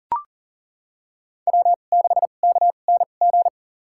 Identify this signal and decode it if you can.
Morse code, heard as a keyed tone, about 40 wpm, spaced as for 29 wpm, 700 Hz. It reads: W6KNG
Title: USA Call Signs - 5 Characters - 2x Character Spacing - 40wpm